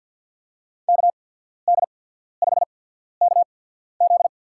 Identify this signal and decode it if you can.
Morse code, heard as a keyed tone, about 53 wpm, spaced as for 14 wpm, 700 Hz. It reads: KD5KZ
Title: USA Call Signs - 5 Characters - 8x Character Spacing - 50wpm